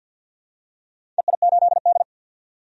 Morse code, heard as a keyed tone, about 51 wpm, 700 Hz. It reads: EI8D
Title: DXCC-100 Call Signs - 4 Characters - 50wpm